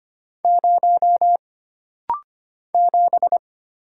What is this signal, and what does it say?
Morse code, heard as a keyed tone, about 25 wpm, 700 Hz. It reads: 0 7
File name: Single Character (Letter:Number:Punctuation:Prosign) - Rapid-Fire - 25wpm